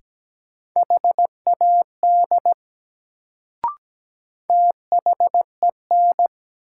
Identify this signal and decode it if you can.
Morse code, heard as a keyed tone, about 17 wpm, 700 Hz. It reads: HAD THEN
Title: Sets of 2 Words - New Word Drill - 17wpm